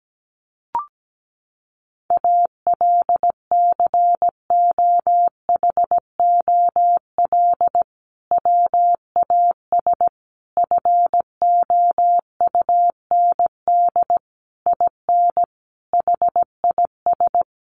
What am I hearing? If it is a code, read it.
Morse code, heard as a keyed tone, about 17 wpm, 700 Hz. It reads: ALCOHOL WAS FOUND IN HIS
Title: Sentences from Top 1400 Words 17wpm